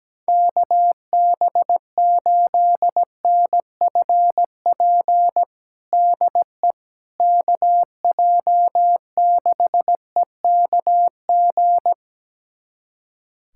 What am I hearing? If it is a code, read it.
Morse code, heard as a keyed tone, about 17 wpm, 700 Hz. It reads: KB8NFP DE KJ6EKG
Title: USA Callsign DE Callsign - 6 Characters - 17wpm